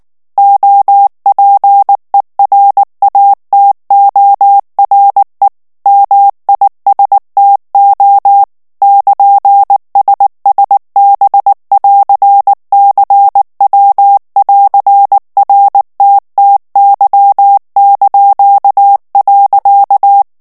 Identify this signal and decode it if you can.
Morse code, heard as a keyed tone, about 19 wpm, 800 Hz. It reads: OPERATORE MISTO <KN>SSB<AR>CW<AR>RTTY).
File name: morse mp3 / ITALIEN texte clair 2 vitesse 18 mots minute